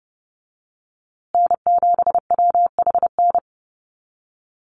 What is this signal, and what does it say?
Morse code, heard as a keyed tone, about 30 wpm, 700 Hz. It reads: N7WHN
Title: USA Call Signs Repeated 2x 30wpm